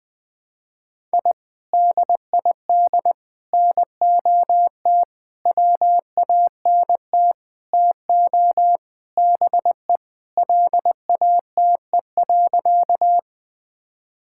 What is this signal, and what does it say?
Morse code, heard as a keyed tone, about 20 wpm, 700 Hz. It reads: I DID NOT WANT TO BE LATE.
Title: Sentences with 5-Word Phrases - Spoken Beginning - 20wpm